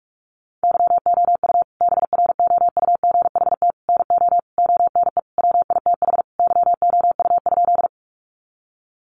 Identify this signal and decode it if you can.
Morse code, heard as a keyed tone, about 44 wpm, 700 Hz. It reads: YOU BROUGHT NO ONE WITH YOU?
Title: Sentences from Top 500 Words 45wpm